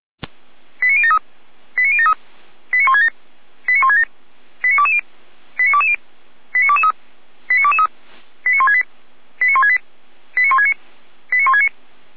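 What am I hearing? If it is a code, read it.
Five-tone selective call (ZVEI), heard as quick runs of five tones, tones of about 70 ms. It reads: 89073 89073 89167 89167 89220 89220 89223 89223 89168 89168 89169 89169
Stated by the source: Received by SDR; ZVEI five-tone code 89060.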